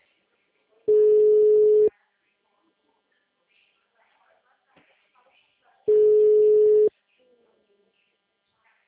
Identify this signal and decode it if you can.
Ringback tone (Europe, 425 Hz), with speech.
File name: out- -2017- - - .57517